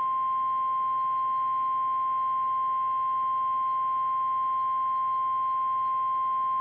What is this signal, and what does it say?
Emergency Alert System (SAME) tones (the 1050 Hz attention tone). It is heard from 0.5 to 5.5 s.
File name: eas / KKPRKXPN / KKPRKXPN